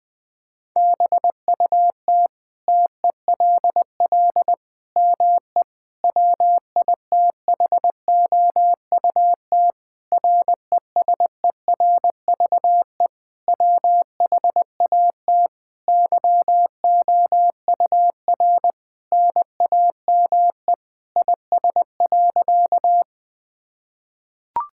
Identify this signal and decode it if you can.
Morse code, heard as a keyed tone, about 20 wpm, 700 Hz. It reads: BUT TELL ME WITHOUT RESERVE WHAT YOUR NAME IS.
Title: Sentences from Top 1000 Words - Review Part 6 - 20wpm